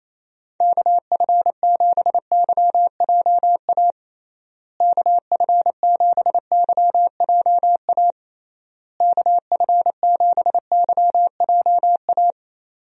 Morse code, heard as a keyed tone, about 28 wpm, 700 Hz. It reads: KF7YJA KF7YJA KF7YJA
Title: USA Call Signs - 6 Characters - Repeated 3x - 28wpm